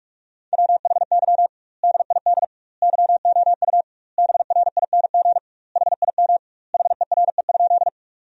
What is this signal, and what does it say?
Morse code, heard as a keyed tone, about 46 wpm, 700 Hz. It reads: WHY DID YOU BRING HIM HERE?
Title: 2x Word-Spacing - Sentences from Top 500 Words - 45wpm